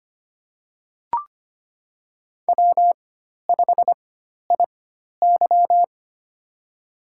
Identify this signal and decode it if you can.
Morse code, heard as a keyed tone, about 25 wpm, spaced as for 12 wpm, 700 Hz. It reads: W5IY
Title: USA Call Signs - 4 Characters - 4x Character Spacing - 25wpm